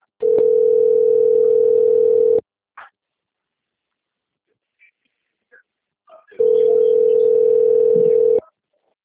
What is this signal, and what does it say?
Ringback tone (North America), 440 + 480 Hz.